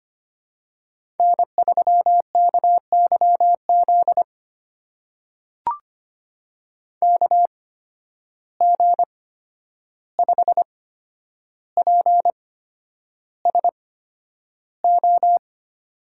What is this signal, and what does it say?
Morse code, heard as a keyed tone, about 25 wpm, spaced as for 10 wpm, 700 Hz. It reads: N3KYZ KG5PSO
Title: USA Call Signs - 8x Character Spacing - 25wpm